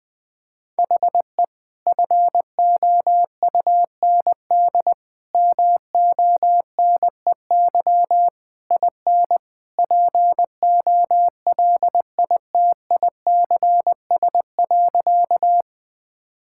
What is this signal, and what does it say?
Morse code, heard as a keyed tone, about 20 wpm, 700 Hz. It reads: HE FOUND MONEY IN POLITICS.
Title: Sentences from Top 1000 Words - Review Part 6 - 20wpm